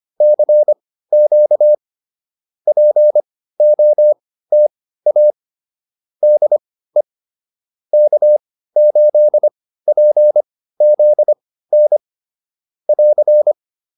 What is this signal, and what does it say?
Morse code, heard as a keyed tone, about 25 wpm, spaced as for 15 wpm, 600 Hz. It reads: CQ POTA DE K8PZN <AR>
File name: POTA201 15